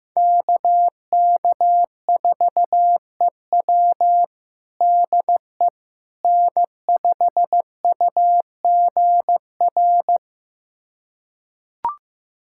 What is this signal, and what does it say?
Morse code, heard as a keyed tone, about 15 wpm, 700 Hz. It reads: KK4EW DE N5UGR
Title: USA Callsign DE Callsign - 5 Characters - 15wpm